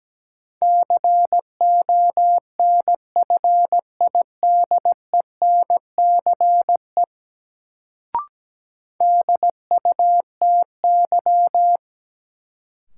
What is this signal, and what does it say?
Morse code, heard as a keyed tone, about 17 wpm, 700 Hz. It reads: CONFIDENCE DUTY
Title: Sentences from Top 1500 Words - New Word Drill - 17wpm